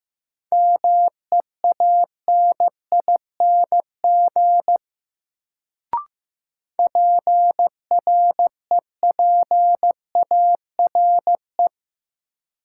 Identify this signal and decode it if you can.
Morse code, heard as a keyed tone, about 16 wpm, 700 Hz. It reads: MEANING PREPARE
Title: Sentences from Top 900 Words - New Word Drill - 15wpm